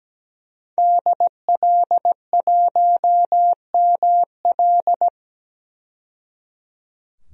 Morse code, heard as a keyed tone, about 17 wpm, 700 Hz. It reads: DL1ML
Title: DXCC-100 Call Signs - 5 Characters - 17wpm